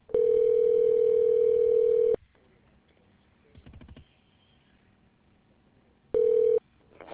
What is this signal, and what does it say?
Ringback tone (North America), 440 + 480 Hz.